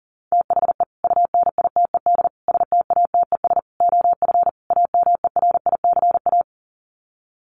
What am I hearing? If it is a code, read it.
Morse code, heard as a keyed tone, about 40 wpm, 700 Hz. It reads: THE UNITED STATES OF AMERICA
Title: Sets of 5 Words 40wpm